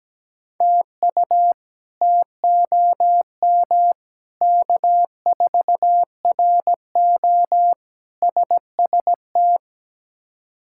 Morse code, heard as a keyed tone, about 17 wpm, 700 Hz. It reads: TU TOM K4RO SST
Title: K1USN SST QSO Practice - 17wpm